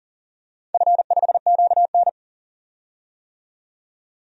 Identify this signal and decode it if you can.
Morse code, heard as a keyed tone, about 40 wpm, 700 Hz. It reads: F5QN
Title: DXCC-100 Call Signs - 4 Characters - 40wpm